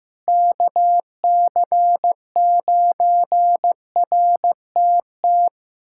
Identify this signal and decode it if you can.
Morse code, heard as a keyed tone, about 15 wpm, 700 Hz. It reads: KC9RTT
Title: USA Callsign DE Callsign - 6 Characters - 15wpm